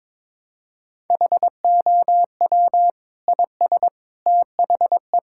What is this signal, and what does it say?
Morse code, heard as a keyed tone, about 22 wpm, 700 Hz. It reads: HOW IS THE
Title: Sentences from Top 1000 Words - Review Part 6 - 22wpm